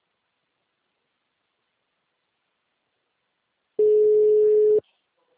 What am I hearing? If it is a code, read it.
Ringback tone (Europe, 425 Hz).